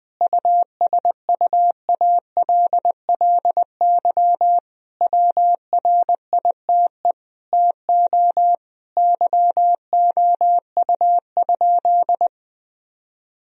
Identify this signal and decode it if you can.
Morse code, heard as a keyed tone, about 20 wpm, 700 Hz. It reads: USUALLY WRITE TO YOU?
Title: Sentences from Top 1000 Words - Review Part 3 - 20wpm